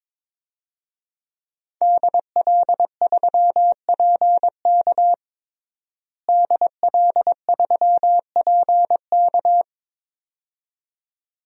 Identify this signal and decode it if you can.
Morse code, heard as a keyed tone, about 22 wpm, 700 Hz. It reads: DL3PK DL3PK
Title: DXCC-100 Call Signs - 5 Characters - Repeated 2x - 22wpm